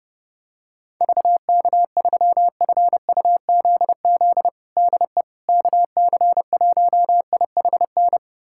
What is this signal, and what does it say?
Morse code, heard as a keyed tone, about 30 wpm, 700 Hz. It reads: VK3FUZZ DE KC1IHN
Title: USA Callsign DE Callsign 30wpm